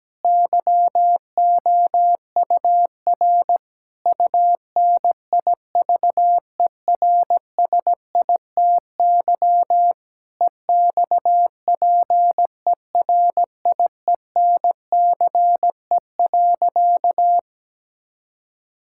Morse code, heard as a keyed tone, about 17 wpm, 700 Hz. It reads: YOUR UNIVERSITY EXPERIENCE.
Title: Sentences from Top 1000 Words - Review Part 5 - 17wpm